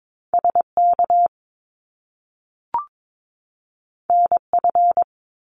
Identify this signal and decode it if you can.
Morse code, heard as a keyed tone, about 22 wpm, 700 Hz. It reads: SK NF
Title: Top Two Letter Combinations in Words 22wpm